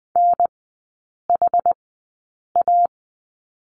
Morse code, heard as a keyed tone, about 20 wpm, spaced as for 8 wpm, 700 Hz. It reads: NHA
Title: 100 Lesson 17e - Letter u - Three Characters 20wpm